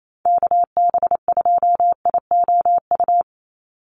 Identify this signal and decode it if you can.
Morse code, heard as a keyed tone, about 28 wpm, 700 Hz. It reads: KB2IOU
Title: USA Call Signs 28wpm